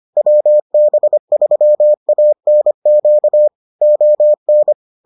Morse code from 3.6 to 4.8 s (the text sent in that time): ON